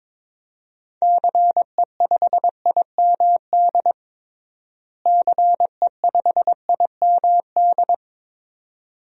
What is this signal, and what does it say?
Morse code, heard as a keyed tone, about 22 wpm, 700 Hz. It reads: CE5IMD CE5IMD
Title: DXCC-100 Call Signs - 6 Characters - Repeated 2x - 22wpm